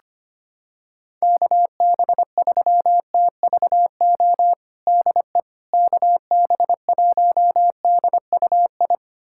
Morse code, heard as a keyed tone, about 25 wpm, 700 Hz. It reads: KB3TVO DE KB1DUI
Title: USA Callsign DE Callsign - 6 Characters - 25wpm